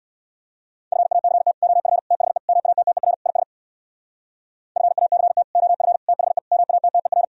Morse code, heard as a keyed tone, about 65 wpm, 700 Hz. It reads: FACT OF THE MATTER IS FACT OF THE MATTER